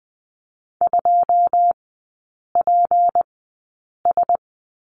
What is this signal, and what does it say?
Morse code, heard as a keyed tone, about 20 wpm, spaced as for 8 wpm, 700 Hz. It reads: 2PS
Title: 152 Lesson 25e - Letter p - Three Characters 20wpm